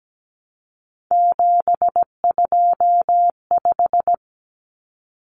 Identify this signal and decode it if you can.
Morse code, heard as a keyed tone, about 17 wpm, 700 Hz. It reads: 725
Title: Common Cut Numbers - 3-Digit 17wpm